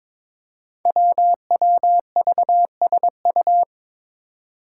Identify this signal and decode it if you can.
Morse code, heard as a keyed tone, about 22 wpm, 700 Hz. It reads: WWVSU
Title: Five Challenging Letter-Number Combinations 22wpm